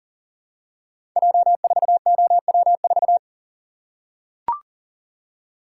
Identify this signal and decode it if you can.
Morse code, heard as a keyed tone, about 39 wpm, 700 Hz. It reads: J4OW4
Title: Code Groups - 5 Letter:Numbers - 40wpm